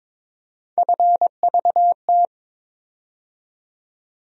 Morse code, heard as a keyed tone, about 22 wpm, 700 Hz. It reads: FVT
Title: USA Call Sign Postfixes - 3 Letters 22wpm